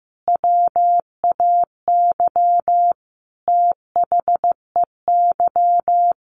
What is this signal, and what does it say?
Morse code, heard as a keyed tone, about 15 wpm, 700 Hz. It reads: WAY THEY
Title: Sentences from Top 1300 Words 15wpm